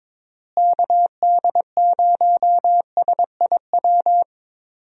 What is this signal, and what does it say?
Morse code, heard as a keyed tone, about 22 wpm, 700 Hz. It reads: KD0SIW
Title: USA Call Signs - 6 Characters - Repeated 2x - 22wpm